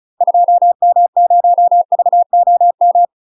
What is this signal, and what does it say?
Morse code, heard as a keyed tone, about 36 wpm, 700 Hz. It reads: 2M0VOM
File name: UK MASTER QSOs TRAINING / 35WPM / UK MASTER QSOs 0031